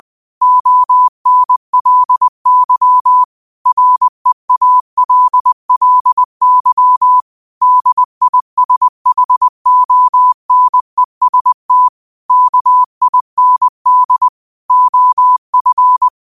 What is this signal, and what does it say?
Morse code, heard as a keyed tone, about 20 wpm, 1000 Hz. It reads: ONLY REALLY DISHONEST KIND OF